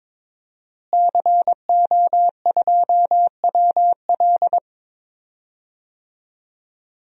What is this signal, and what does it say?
Morse code, heard as a keyed tone, about 22 wpm, 700 Hz. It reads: CO2WL
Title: DXCC-100 Call Signs - 5 Characters - Repeated 3x - 22wpm